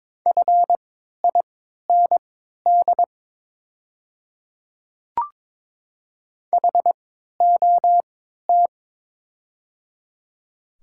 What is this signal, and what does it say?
Morse code, heard as a keyed tone, about 22 wpm, spaced as for 13 wpm, 700 Hz. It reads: FIND HOT
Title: 3x Character Spacing - Top 100 Words - 22wpm